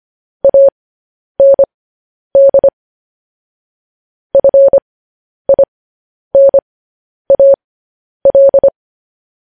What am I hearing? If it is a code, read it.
Morse code, heard as a keyed tone, about 25 wpm, spaced as for 10 wpm, 550 Hz. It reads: AND FINAL
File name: podcast / News headlines-1x-25x10